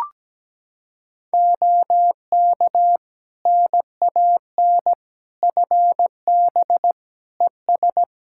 Morse code, heard as a keyed tone, about 17 wpm, 700 Hz. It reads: OK NAN FB ES
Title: Practice QSO - Exchange 2 - Info - 17wpm